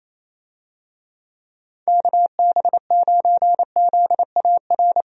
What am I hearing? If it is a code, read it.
Morse code, heard as a keyed tone, about 28 wpm, 700 Hz. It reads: KB9ZAR
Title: USA Call Signs - 6 Characters - Repeated 3x - 28wpm